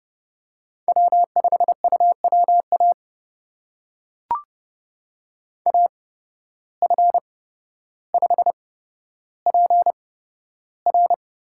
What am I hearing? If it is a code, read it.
Morse code, heard as a keyed tone, about 30 wpm, spaced as for 13 wpm, 700 Hz. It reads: W5UWA AF5PR
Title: USA Call Signs - 5 Characters - 8x Character Spacing - 30wpm